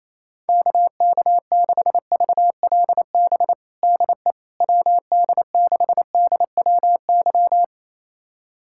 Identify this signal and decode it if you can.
Morse code, heard as a keyed tone, about 28 wpm, 700 Hz. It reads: KK6VLB DE WD6DWY
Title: USA Callsign DE Callsign - 6 Characters - 28wpm